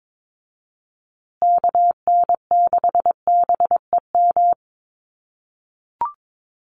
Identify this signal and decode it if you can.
Morse code, heard as a keyed tone, about 22 wpm, 700 Hz. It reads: KN6BEM
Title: USA Call Signs Repeated 3x 22wpm